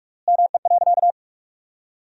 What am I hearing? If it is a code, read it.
Morse code, heard as a keyed tone, about 46 wpm, 700 Hz. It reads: ME.
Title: 4x Word-Spacing - Sentences from Top 200 Words - 45wpm